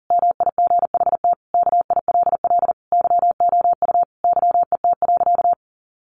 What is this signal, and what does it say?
Morse code, heard as a keyed tone, about 37 wpm, 700 Hz. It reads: MIGHT KILL YOU YET.
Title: A Sound Of Thunder 40wpm